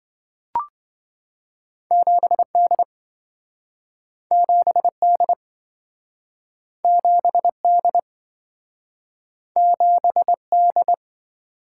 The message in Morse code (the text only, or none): 7D 7D 7D 7D